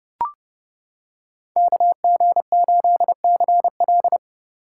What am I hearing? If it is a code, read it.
Morse code, heard as a keyed tone, about 29 wpm, 700 Hz. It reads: KG8CL